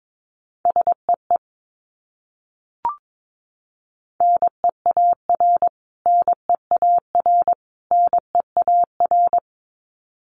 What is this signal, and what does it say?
Morse code, heard as a keyed tone, about 22 wpm, 700 Hz. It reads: SEE NEAR NEAR NEAR
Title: Top 300 Words Repeated 22wpm